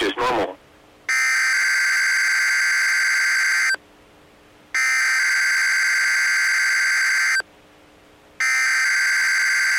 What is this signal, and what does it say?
Emergency Alert System (SAME) tones (header bursts), heard as harsh buzzing bursts, with speech. It is heard from 1.1 to 8.7 s.